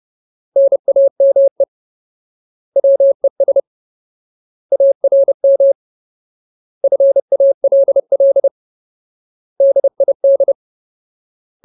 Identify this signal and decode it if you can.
Morse code, heard as a keyed tone, about 30 wpm, 550 Hz. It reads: NAME WES ARM FALL DID
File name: Ultimate Word List / 30WPM Ultimate